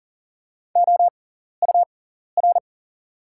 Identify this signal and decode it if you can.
Morse code, heard as a keyed tone, about 40 wpm, spaced as for 14 wpm, 700 Hz. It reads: OUR
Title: 6x Character Spacing - Top 300 Words - 40wpm